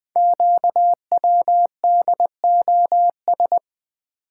Morse code, heard as a keyed tone, about 20 wpm, 700 Hz. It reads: QWDOS
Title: Code Groups - 5 Letters - 20wpm